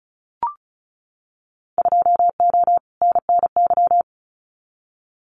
Morse code, heard as a keyed tone, about 34 wpm, 700 Hz. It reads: 2O NNY
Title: Winter Field Day Exchanges 35wpm